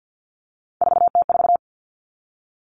Morse code, heard as a keyed tone, about 50 wpm, 700 Hz. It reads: 4T4
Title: Three Challenging Letter-Number Combinations 50wpm